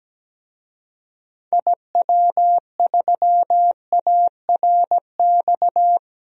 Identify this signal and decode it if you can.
Morse code, heard as a keyed tone, about 17 wpm, 700 Hz. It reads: IW3ARX
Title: DXCC-100 Call Signs - 6 Characters - Repeated 3x - 17wpm